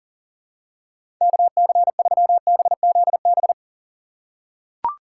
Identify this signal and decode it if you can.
Morse code, heard as a keyed tone, about 40 wpm, 690 Hz. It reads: KC3BZB
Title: USA Call Signs - 6 Characters - 2x Character Spacing - 40wpm